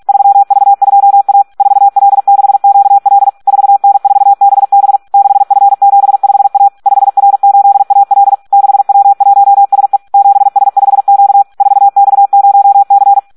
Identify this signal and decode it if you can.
Morse code, heard as a keyed tone, about 46 wpm, 800 Hz. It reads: 3V2A 4L6QL 4N4BB 6F75A 5R8AL 6W1SE 7S5Q 4X0C
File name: Audio practica CW / SD 1 46